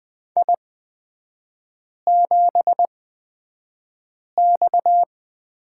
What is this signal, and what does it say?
Morse code, heard as a keyed tone, about 20 wpm, spaced as for 5 wpm, 690 Hz. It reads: I7X
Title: 229 Lesson 37e - Letter x - Three Characters 20wpm